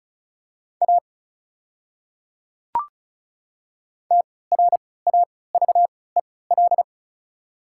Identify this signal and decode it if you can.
Morse code, heard as a keyed tone, about 34 wpm, spaced as for 20 wpm, 690 Hz. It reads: A TRAVEL
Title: 3x Character Spacing - Top 500 Words - 35wpm